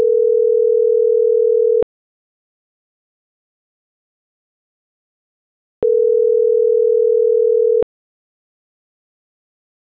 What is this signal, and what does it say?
Ringback tone (North America), 440 + 480 Hz.